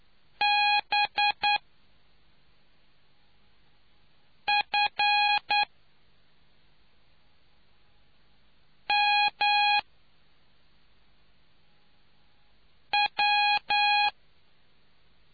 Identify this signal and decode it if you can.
Morse code, heard as a keyed tone, about 9 wpm, spaced as for 3 wpm, 830 Hz. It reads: BFMW